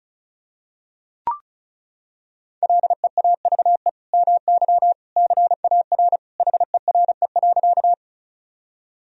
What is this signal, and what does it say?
Morse code, heard as a keyed tone, about 34 wpm, 700 Hz. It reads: LEAVE MY CAR HERE.
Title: Sentences from Top 1000 Words - Review Part 1 - 35wpm